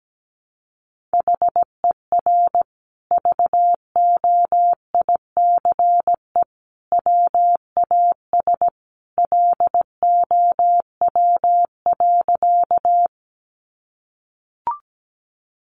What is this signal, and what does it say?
Morse code, heard as a keyed tone, about 17 wpm, 700 Hz. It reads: HER VOICE WAS LOW.
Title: Sentences from Top 500 Words 17wpm